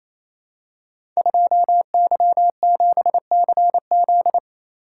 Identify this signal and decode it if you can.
Morse code, heard as a keyed tone, about 28 wpm, 700 Hz. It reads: 2Y7CZ
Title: Code Groups - 5 Letter:Numbers - 28wpm